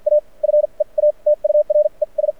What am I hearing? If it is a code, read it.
Morse code, heard as a keyed tone, about 52 wpm, spaced as for 38 wpm, 590 Hz. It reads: A FEATURES
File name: archived / CRSnet